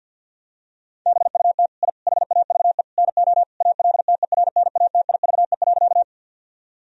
Morse code, heard as a keyed tone, about 51 wpm, 700 Hz. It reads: BUT I HAVE NO ALTERNATIVE.
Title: Sentences from Top 1000 Words - Review Part 4 - 50wpm